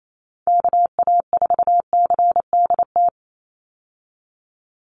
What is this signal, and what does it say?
Morse code, heard as a keyed tone, about 28 wpm, 700 Hz. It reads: KA4CDT
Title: USA Call Signs 28wpm